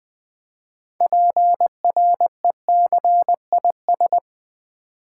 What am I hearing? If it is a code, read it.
Morse code, heard as a keyed tone, about 20 wpm, 700 Hz. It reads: PRECIS
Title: Build A Word - 3 Letters At A Time - 9-Letter Words - 20wpm